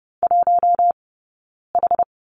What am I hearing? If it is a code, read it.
Morse code, heard as a keyed tone, about 30 wpm, spaced as for 9 wpm, 690 Hz. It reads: 1H
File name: Lesson 19d - Letter w - Two Characters 30wpm